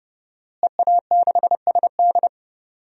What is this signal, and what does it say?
Morse code, heard as a keyed tone, about 30 wpm, 700 Hz. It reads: EA6SD
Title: DXCC-100 Call Signs - 5 Characters - Repeated 3x - 30wpm